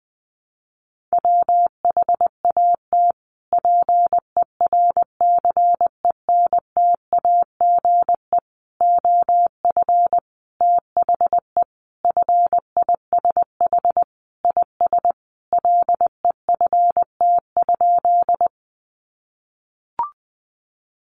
Morse code, heard as a keyed tone, about 20 wpm, 700 Hz. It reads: WHAT PERCENTAGE OF THE FISH IS LEFT?
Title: Sentences from Top 1300 Words 20wpm